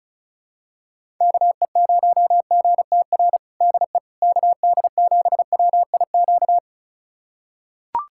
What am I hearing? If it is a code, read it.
Morse code, heard as a keyed tone, about 36 wpm, 700 Hz. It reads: KE0GTR DE KD7WIQ
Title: USA Callsign DE Callsign - 6 Characters - 35wpm